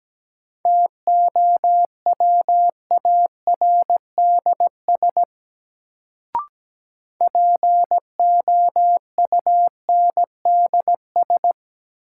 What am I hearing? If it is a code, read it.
Morse code, heard as a keyed tone, about 17 wpm, 700 Hz. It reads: TOWARDS POUNDS
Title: Sentences from Top 1000 Words - Review Part 2 - New Word Drill - 17wpm